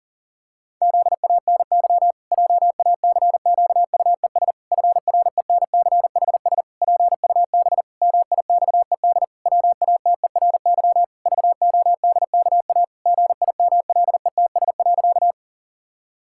Morse code, heard as a keyed tone, about 37 wpm, 700 Hz. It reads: ZANY JACQUES FRENCHS PUB MIXED WATERY VODKA GIMLETS.
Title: Pangram Challenge v2 - 40wpm